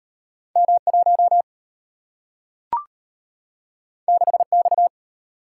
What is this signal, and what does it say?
Morse code, heard as a keyed tone, about 34 wpm, 700 Hz. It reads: M1 6X